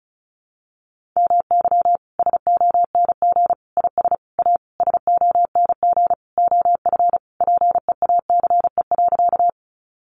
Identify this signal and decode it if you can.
Morse code, heard as a keyed tone, about 34 wpm, 700 Hz. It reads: MY SONG IS A SONG OF PEACE.
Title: Sentences from Top 1500 Words 35wpm